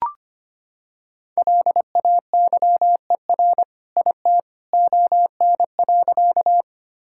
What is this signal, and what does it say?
Morse code, heard as a keyed tone, about 25 wpm, 700 Hz. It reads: LAYER IT ON.
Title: Sentences from Top 1600 Words 25wpm